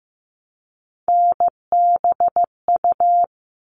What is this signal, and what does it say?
Morse code, heard as a keyed tone, about 15 wpm, 700 Hz. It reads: NBU
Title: All Cut Numbers - 3-Digit 15wpm